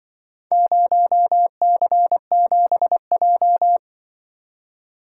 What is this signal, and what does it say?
Morse code, heard as a keyed tone, about 24 wpm, 700 Hz. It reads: 0C7J
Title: Speed-Racing - Four Letter-Number Combinations - 20wpm